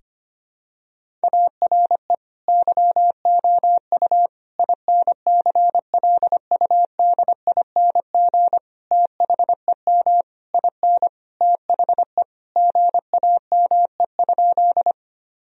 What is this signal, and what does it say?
Morse code, heard as a keyed tone, about 25 wpm, 700 Hz. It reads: ARE YOU INCLUDING THEM IN THE GAME?
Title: Sentences from Top 1000 Words - Review Part 4 - 25wpm